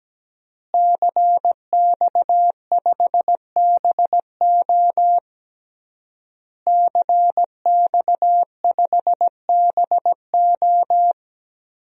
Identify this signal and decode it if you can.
Morse code, heard as a keyed tone, about 17 wpm, 700 Hz. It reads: CX5BO CX5BO
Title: DXCC-100 Call Signs - 5 Characters - Repeated 3x - 17wpm